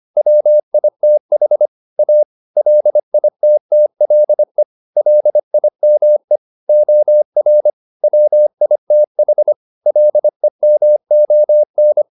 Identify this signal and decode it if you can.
Morse code, heard as a keyed tone, about 25 wpm, 600 Hz. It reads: WITH A LITTLE LIME OR WITH LEMON
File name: the old man and the sea part 1 25wpm